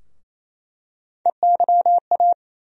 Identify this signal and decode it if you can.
Morse code, heard as a keyed tone, about 27 wpm, 700 Hz. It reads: EYA
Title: USA Call Sign Postfixes - 3 Letters 28wpm